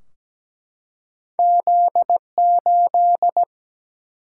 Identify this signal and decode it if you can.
Morse code, heard as a keyed tone, about 17 wpm, 700 Hz. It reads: Z8
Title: Single-Letter Single-Number 17wpm